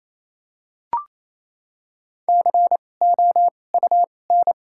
Morse code, heard as a keyed tone, about 28 wpm, spaced as for 20 wpm, 700 Hz. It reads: COUN
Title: 2x Character Spacing - Top 300 Words - 28wpm